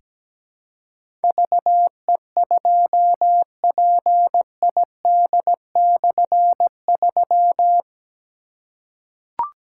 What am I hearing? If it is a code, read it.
Morse code, heard as a keyed tone, about 17 wpm, 700 Hz. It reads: VE2PID/3
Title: K1USN SST QSO Practice - 17wpm